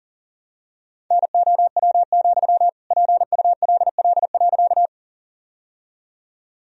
Morse code, heard as a keyed tone, about 40 wpm, 700 Hz. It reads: NOW, PULL.
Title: Sentences from Top 1000 Words - Review Part 2 - 40wpm